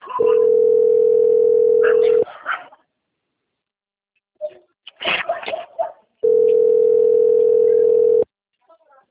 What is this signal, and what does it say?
Ringback tone (North America), 440 + 480 Hz, with speech.
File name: out- - - - - .38118